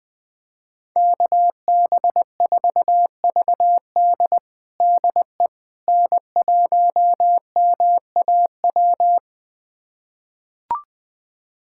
Morse code, heard as a keyed tone, about 21 wpm, 700 Hz. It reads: KB4VD DE N1MAW
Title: USA Callsign DE Callsign - 5 Characters - 20wpm